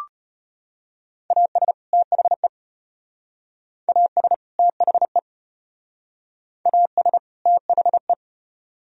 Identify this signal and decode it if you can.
Morse code, heard as a keyed tone, about 31 wpm, 700 Hz. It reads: AS THE AS THE AS THE
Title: Speed-Racing - Sets of 2 Words 25wpm